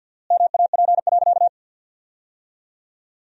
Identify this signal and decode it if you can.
Morse code, heard as a keyed tone, about 51 wpm, 700 Hz. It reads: MAP.
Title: 4x Word-Spacing - Sentences from Top 400 Words - 50wpm